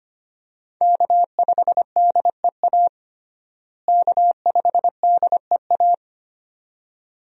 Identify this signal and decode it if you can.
Morse code, heard as a keyed tone, about 25 wpm, 700 Hz. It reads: K5DEA K5DEA
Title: USA Call Signs - 5 Characters - Repeated 2x - 25wpm